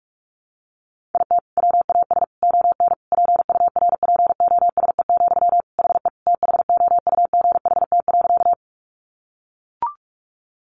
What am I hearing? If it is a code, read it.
Morse code, heard as a keyed tone, about 44 wpm, 700 Hz. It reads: IT WAS ON PURPOSE, HE THOUGHT.
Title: Sentences from Top 800 Words 45wpm